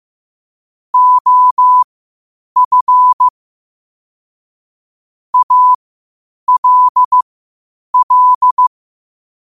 Morse code, heard as a keyed tone, about 15 wpm, spaced as for 8 wpm, 1000 Hz. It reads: OF ALL
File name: qotdfiles / 8wpm